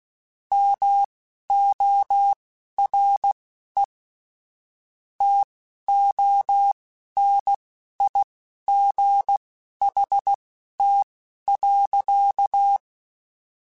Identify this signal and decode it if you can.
Morse code, heard as a keyed tone, about 16 wpm, spaced as for 11 wpm, 800 Hz. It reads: MORE TONIGHT.
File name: (spaced) / QSOs Spaced-mp3s / WA7QOX 1 @15 sp